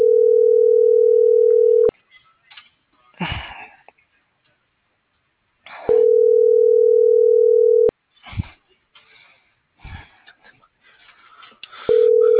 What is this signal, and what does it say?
Ringback tone (North America), 440 + 480 Hz.